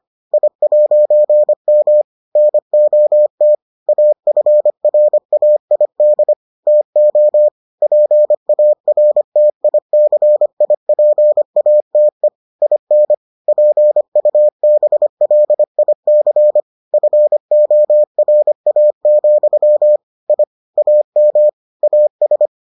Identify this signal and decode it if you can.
Morse code, heard as a keyed tone, about 25 wpm, 600 Hz. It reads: I'M NOT AFRAID TO PARTICIPATE IN PUBLIC FORA, I AM AS